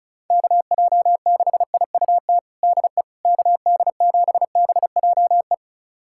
Morse code, heard as a keyed tone, about 34 wpm, 700 Hz. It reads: KJ6IUT DE KD7BJE
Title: USA Callsign DE Callsign 35wpm